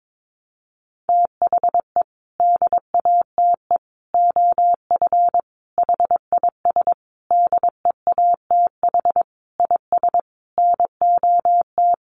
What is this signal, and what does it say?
Morse code, heard as a keyed tone, about 22 wpm, 700 Hz. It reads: THE DATE OF HIS DEATH IS NOT
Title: Sentences from Top 1100 Words 22wpm